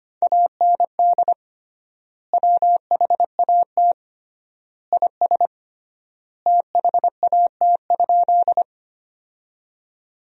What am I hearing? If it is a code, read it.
Morse code, heard as a keyed tone, about 25 wpm, 700 Hz. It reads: AND WHAT IS THAT?
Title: 3x Word-Spacing - Sentences from Top 100 Words - 25wpm